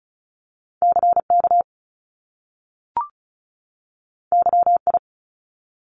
Morse code, heard as a keyed tone, about 36 wpm, 700 Hz. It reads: CK YI
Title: Top Two Letter Combinations in Words 35wpm